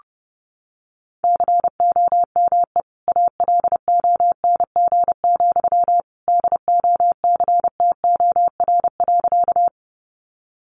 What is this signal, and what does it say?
Morse code, heard as a keyed tone, about 30 wpm, 700 Hz. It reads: COME ALONG, DOCTOR.
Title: Sentences from Top 1500 Words 30wpm